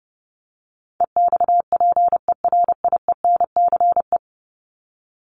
Morse code, heard as a keyed tone, about 30 wpm, 700 Hz. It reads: EXPERIENCE
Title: Top 501-750 Words 30wpm